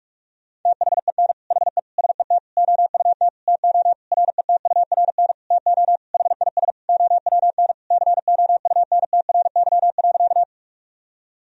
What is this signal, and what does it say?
Morse code, heard as a keyed tone, about 46 wpm, 700 Hz. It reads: THEN HE SET OUT TO RETURN TO HIS OWN COUNTRY.
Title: Sentences from Top 1000 Words - Review Part 1 - 45wpm